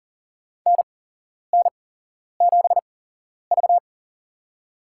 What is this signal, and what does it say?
Morse code, heard as a keyed tone, about 40 wpm, spaced as for 11 wpm, 690 Hz. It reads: NN7V
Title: USA Call Signs - 4 Characters - 8x Character Spacing - 40wpm